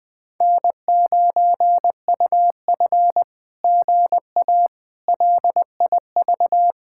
Morse code, heard as a keyed tone, about 20 wpm, 700 Hz. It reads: N9UF GA LIV
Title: SOTA QSO Practice 20wpm